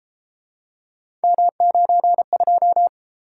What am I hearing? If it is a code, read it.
Morse code, heard as a keyed tone, about 31 wpm, 700 Hz. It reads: M92